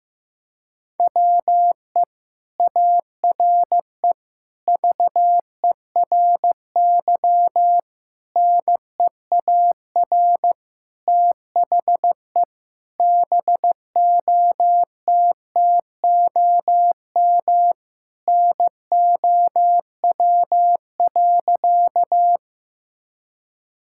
Morse code, heard as a keyed tone, about 15 wpm, 700 Hz. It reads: WE ARE VERY NEAR THE BOTTOM NOW.
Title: Sentences from Top 1000 Words - Review Part 6 - 15wpm